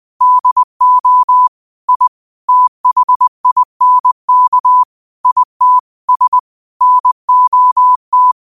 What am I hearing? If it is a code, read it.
Morse code, heard as a keyed tone, about 20 wpm, 1000 Hz. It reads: DO I THINK IT S NOT
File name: qotdfiles / 20wpm